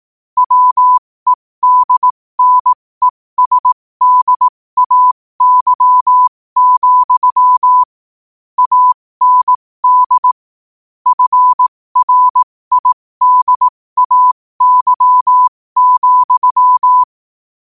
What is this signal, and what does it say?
Morse code, heard as a keyed tone, about 18 wpm, spaced as for 15 wpm, 1000 Hz. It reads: WEDNESDAY, AND FRIDAY,
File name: qotdfiles / 15wpm